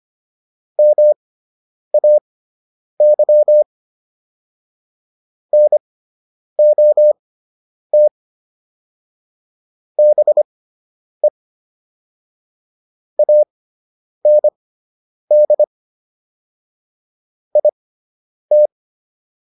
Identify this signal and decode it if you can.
Morse code, heard as a keyed tone, about 25 wpm, spaced as for 9 wpm, 600 Hz. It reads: MAY NOT BE AND IT